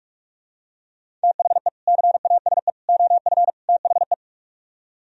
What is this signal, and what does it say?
Morse code, heard as a keyed tone, about 44 wpm, 700 Hz. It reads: THE CASE OF THE